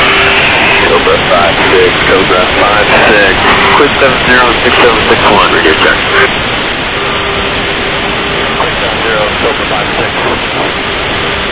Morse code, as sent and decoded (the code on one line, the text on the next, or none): none